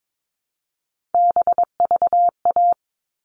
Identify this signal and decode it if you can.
Morse code, heard as a keyed tone, about 22 wpm, 700 Hz. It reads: BVA
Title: All Cut Numbers - 3-Digit 22wpm